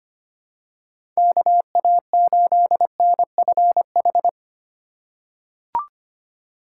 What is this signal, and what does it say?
Morse code, heard as a keyed tone, about 25 wpm, 700 Hz. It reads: KA8NFH
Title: USA Call Signs - 6 Characters - 3x Character Spacing - 25wpm